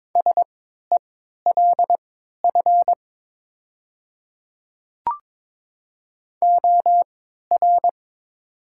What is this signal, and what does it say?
Morse code, heard as a keyed tone, about 22 wpm, spaced as for 12 wpm, 700 Hz. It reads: SELF OR
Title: 3x Character Spacing - Top 300 Words - 22wpm